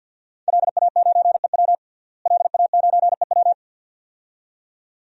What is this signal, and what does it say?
Morse code, heard as a keyed tone, about 51 wpm, 700 Hz. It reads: LA9EW LA9EW
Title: DXCC-100 Call Signs - 5 Characters - Repeated 2x - 50wpm